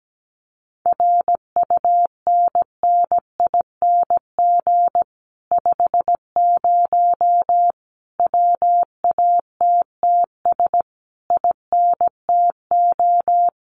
Morse code, heard as a keyed tone, about 17 wpm, 700 Hz. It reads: RUNNING 50 WATTS INTO
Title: Common QSO Phrases Encore 17wpm